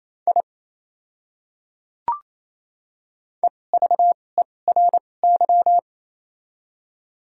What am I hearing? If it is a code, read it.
Morse code, heard as a keyed tone, about 27 wpm, spaced as for 20 wpm, 700 Hz. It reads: I EVERY
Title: 2x Character Spacing - Top 200 Words - 28wpm